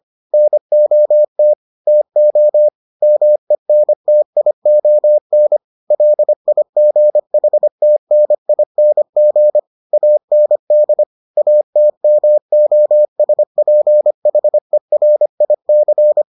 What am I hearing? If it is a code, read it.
Morse code, heard as a keyed tone, about 25 wpm, 600 Hz. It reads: NOT TO MENTION LIGHTNING AND ATMOSPHERIC